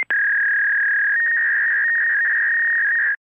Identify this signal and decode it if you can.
Fax machine handshake.